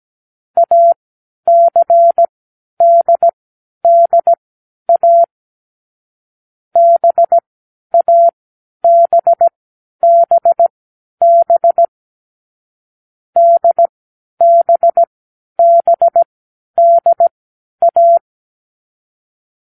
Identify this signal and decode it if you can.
Morse code, heard as a keyed tone, about 17 wpm, spaced as for 10 wpm, 680 Hz. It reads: ACDDA BABBB DBBDA
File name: k7qo / Lesson6(A-D)